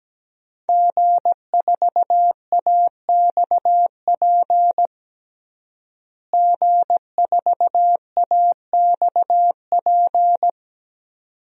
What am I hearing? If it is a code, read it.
Morse code, heard as a keyed tone, about 17 wpm, 700 Hz. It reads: G4AXP G4AXP
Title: DXCC-100 Call Signs - 5 Characters - Repeated 2x - 17wpm